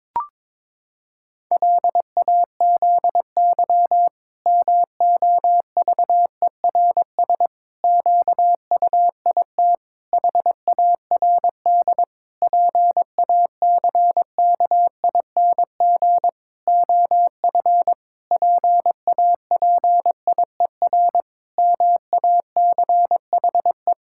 Morse code, heard as a keyed tone, about 22 wpm, 700 Hz. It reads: LAZY MOVERS QUIT HARD PACKING OF PAPIER MACHE
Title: Pangram Challenge v2 - 22wpm